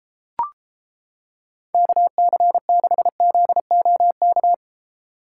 Morse code, heard as a keyed tone, about 31 wpm, 700 Hz. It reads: KC6ZOK